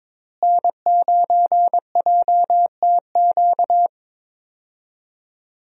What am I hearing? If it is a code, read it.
Morse code, heard as a keyed tone, about 22 wpm, 700 Hz. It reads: N9JTQ
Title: USA Call Signs - 5 Characters - Repeated 3x - 22wpm